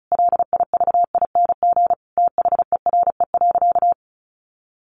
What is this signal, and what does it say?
Morse code, heard as a keyed tone, about 34 wpm, 700 Hz. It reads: LIVING THERE.
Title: Sentences from Top 1500 Words 35wpm